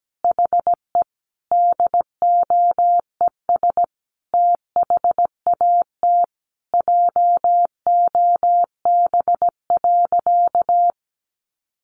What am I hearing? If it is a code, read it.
Morse code, heard as a keyed tone, about 17 wpm, 700 Hz. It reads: HE DOES THAT JOB.
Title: Sentences from Top 700 Words 17wpm